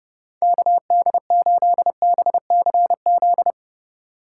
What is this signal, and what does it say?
Morse code, heard as a keyed tone, about 29 wpm, 700 Hz. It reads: KD8BCZ